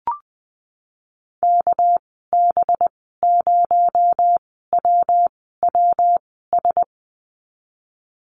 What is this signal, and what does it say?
Morse code, heard as a keyed tone, about 20 wpm, spaced as for 15 wpm, 700 Hz. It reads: KB0WWS
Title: 219 Lesson 35g - Number 0 - Call Signs 20wpm